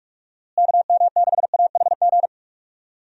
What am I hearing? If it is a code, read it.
Morse code, heard as a keyed tone, about 46 wpm, 700 Hz. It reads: KM6AHG
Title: USA Call Signs - 6 Characters - Repeated 2x - 45wpm